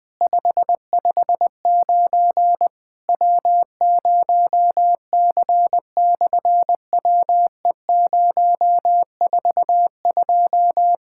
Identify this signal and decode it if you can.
Morse code, heard as a keyed tone, about 20 wpm, 700 Hz. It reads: 559 W0C/WE042
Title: SOTA QSO Practice 20wpm